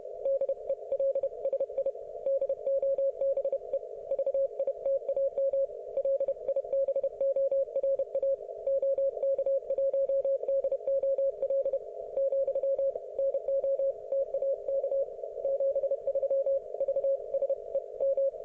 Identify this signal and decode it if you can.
Morse code, heard as a keyed tone, about 31 wpm, 530 Hz. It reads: DELSI DOBE VITAM LIBORA OK1DOL , NOTAK 73 VSE M